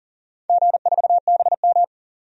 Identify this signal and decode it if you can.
Morse code, heard as a keyed tone, about 40 wpm, 700 Hz. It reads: G4BM
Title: DXCC-100 Call Signs - 4 Characters - 40wpm